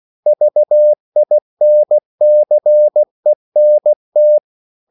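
Morse code, heard as a keyed tone, about 16 wpm, 600 Hz. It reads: VINCENT
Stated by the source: CW practice of national capitals at 16 words per minute, from ON6ZQ.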